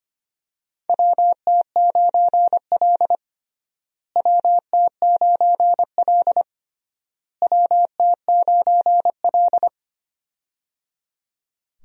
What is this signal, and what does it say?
Morse code, heard as a keyed tone, about 25 wpm, 700 Hz. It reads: WT9L WT9L WT9L
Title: USA Call Signs - 4 Characters - Repeated 3x - 25wpm